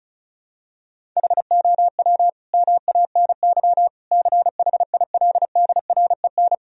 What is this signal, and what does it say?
Morse code, heard as a keyed tone, about 34 wpm, 700 Hz. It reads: HOW MANY CHILDREN
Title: Sentences from Top 1000 Words - Review Part 1 - 35wpm